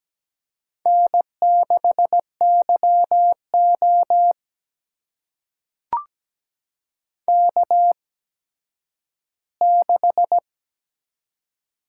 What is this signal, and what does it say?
Morse code, heard as a keyed tone, about 17 wpm, spaced as for 10 wpm, 700 Hz. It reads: N6YO K6
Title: USA Call Signs - 4 Characters - 8x Character Spacing - 17wpm